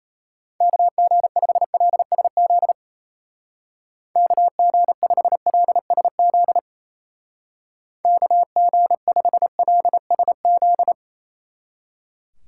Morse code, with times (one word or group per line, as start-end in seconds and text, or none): 0.6-2.7 KG5LSZ
4.2-6.6 KG5LSZ
8.0-10.9 KG5LSZ